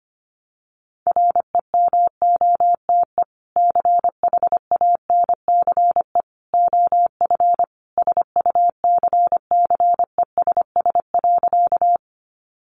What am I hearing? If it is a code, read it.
Morse code, heard as a keyed tone, about 25 wpm, 700 Hz. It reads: REMOTE CHANCE OF SUCCESS.
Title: Sentences from Top 1300 Words 25wpm